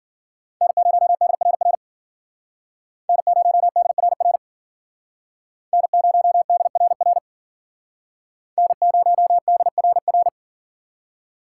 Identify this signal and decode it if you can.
Morse code, heard as a keyed tone, about 52 wpm, 700 Hz. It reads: N0DRR N0DRR N0DRR N0DRR